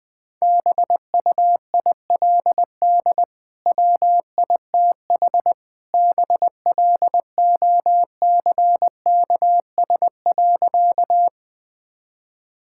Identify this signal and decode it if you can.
Morse code, heard as a keyed tone, about 20 wpm, 700 Hz. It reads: BUILD WITH BLOCKS.
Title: Easy Sentences with Hints - 20wpm